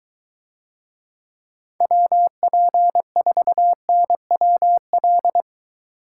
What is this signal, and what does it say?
Morse code, heard as a keyed tone, about 22 wpm, 700 Hz. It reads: WP4NWL